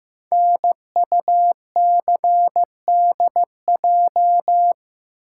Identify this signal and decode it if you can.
Morse code, heard as a keyed tone, about 15 wpm, 700 Hz. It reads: NUCDJ
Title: Code Groups - 5 Letters - 15wpm